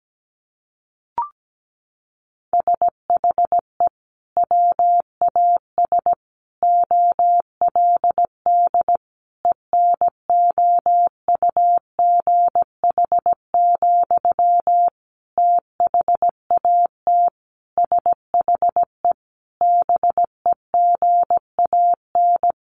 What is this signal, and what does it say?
Morse code, heard as a keyed tone, about 17 wpm, 700 Hz. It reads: SHE WAS OLD ENOUGH, THAT SHE BEGAN
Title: Sentences from Top 800 Words 17wpm